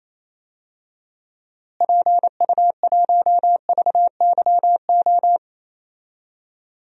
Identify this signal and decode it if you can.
Morse code, heard as a keyed tone, about 28 wpm, 700 Hz. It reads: PU1VYO
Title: DXCC-100 Call Signs - 6 Characters - Repeated 2x - 28wpm